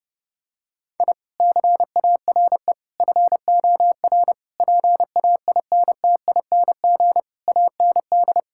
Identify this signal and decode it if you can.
Morse code, heard as a keyed tone, about 30 wpm, 700 Hz. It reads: I CARE FOR PAINTING AND
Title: Sentences from Top 1000 Words - Review Part 6 - 30wpm